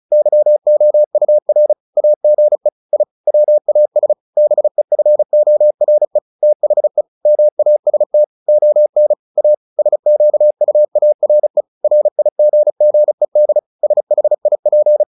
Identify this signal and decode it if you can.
Morse code, heard as a keyed tone, about 34 wpm, 600 Hz. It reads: YOUR AGE I WAS BEFORE THE MAST ON A SQUARE RIGGED SHIP
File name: the old man and the sea / the old man and the sea 1 35wpm